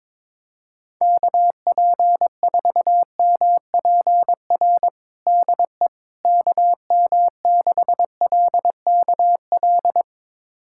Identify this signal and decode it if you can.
Morse code, heard as a keyed tone, about 22 wpm, 700 Hz. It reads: KP4MPR DE KM6LKL
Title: USA Callsign DE Callsign - 6 Characters - 22wpm